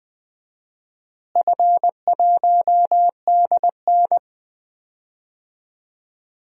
Morse code, heard as a keyed tone, about 20 wpm, 700 Hz. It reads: F1DN
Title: DXCC-100 Call Signs - 4 Characters - Repeated 3x - 20wpm